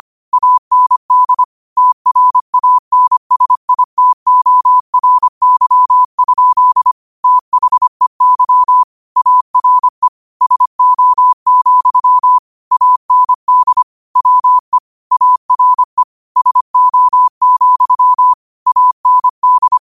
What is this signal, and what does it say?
Morse code, heard as a keyed tone, about 25 wpm, 1000 Hz. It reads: AND TRANSITORY? THEY ARE SO, AND WE ARE SO, AND